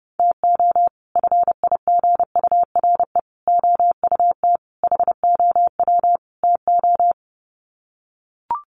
Morse code, heard as a keyed tone, about 28 wpm, 700 Hz. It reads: TO FIGURE OUT HOW TO
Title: Sets of 5 Words 30wpm